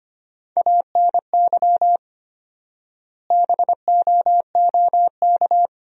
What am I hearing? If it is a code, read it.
Morse code, heard as a keyed tone, about 25 wpm, 700 Hz. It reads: ANY BOOK
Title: 4x Word-Spacing - Sentences from Top 300 Words - 25wpm